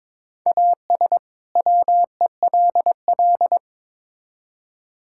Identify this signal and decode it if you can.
Morse code, heard as a keyed tone, about 22 wpm, 700 Hz. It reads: AS WELL
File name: Build A Phrase - 2-Word Phrases - 22wpm